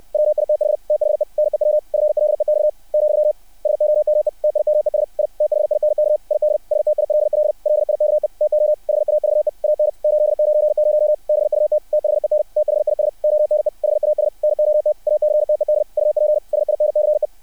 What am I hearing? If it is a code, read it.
Morse code, heard as a keyed tone, about 60 wpm, 600 Hz. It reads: WITH THE NEW FLEX , SOME TIMES I THINK ID STILL LIKE TO HAVE AN 890 BUT THEN THIS ONE HAS NOT ACTED UP SINCE
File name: archived / CRSnet